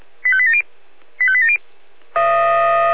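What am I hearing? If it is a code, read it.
Five-tone selective call (ZVEI), heard as quick runs of five tones, tones of about 70 ms. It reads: 85870 85870
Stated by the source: ZVEI five-tone selective call received on SDR.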